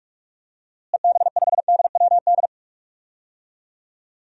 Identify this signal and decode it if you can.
Morse code, heard as a keyed tone, about 46 wpm, 700 Hz. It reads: EB5DWD
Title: DXCC-100 Call Signs - 6 Characters - Repeated 3x - 45wpm